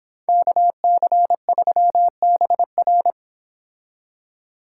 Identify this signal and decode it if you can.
Morse code, heard as a keyed tone, about 29 wpm, 700 Hz. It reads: KC3BR